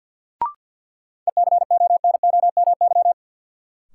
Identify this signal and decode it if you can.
Morse code, heard as a keyed tone, about 53 wpm, 700 Hz. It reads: ECONOMY
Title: Sentences from Top 1000 Words - Review Part 3 - New Word Drill - 50wpm